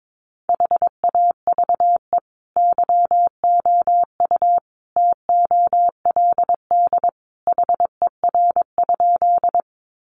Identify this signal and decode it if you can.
Morse code, heard as a keyed tone, about 22 wpm, 700 Hz. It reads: HAVE YOU TOLD HER?
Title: Sentences from Top 400 Words 22wpm